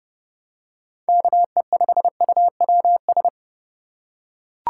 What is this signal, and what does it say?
Morse code, heard as a keyed tone, about 30 wpm, 700 Hz. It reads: KE5UWS